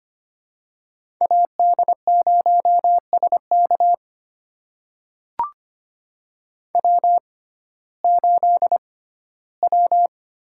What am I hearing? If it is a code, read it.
Morse code, heard as a keyed tone, about 25 wpm, spaced as for 15 wpm, 700 Hz. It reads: AD0SK W8W
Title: USA Call Signs - 5 Characters - 6x Character Spacing - 25wpm